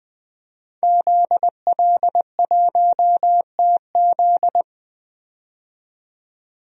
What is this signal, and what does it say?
Morse code, heard as a keyed tone, about 20 wpm, 700 Hz. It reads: ZL1TZ
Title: DXCC-100 Call Signs - 5 Characters - 20wpm